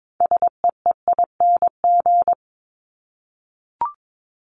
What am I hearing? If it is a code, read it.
Morse code, heard as a keyed tone, about 22 wpm, 700 Hz. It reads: SEEING
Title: ING Suffix - Top 100 Words with Base Words 22wpm